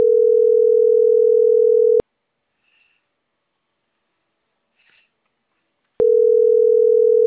Ringback tone (North America), 440 + 480 Hz.